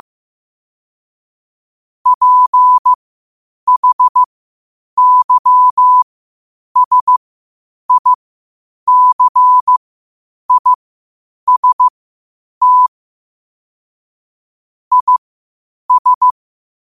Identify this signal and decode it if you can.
Morse code, heard as a keyed tone, about 15 wpm, spaced as for 8 wpm, 1000 Hz. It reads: PHYSICIST IS
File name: qotdfiles / 8wpm